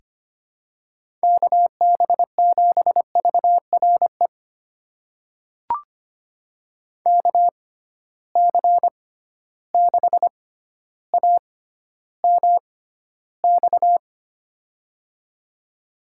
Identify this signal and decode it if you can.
Morse code, heard as a keyed tone, about 25 wpm, spaced as for 13 wpm, 700 Hz. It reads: KB7VRE KC6AMX
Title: USA Call Signs - 6 Characters - 6x Character Spacing - 25wpm